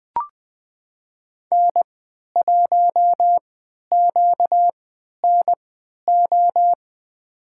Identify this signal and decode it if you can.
Morse code, heard as a keyed tone, about 20 wpm, spaced as for 11 wpm, 700 Hz. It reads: N1QNO
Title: USA Call Signs - 5 Characters - 3x Character Spacing - 20wpm